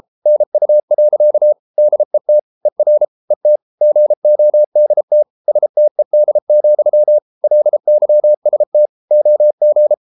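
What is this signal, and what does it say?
Morse code, heard as a keyed tone, about 33 wpm, 600 Hz. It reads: NU. DET ER ET GODT STED, LYST OG
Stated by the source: Morse code at 33 wpm, QRQ.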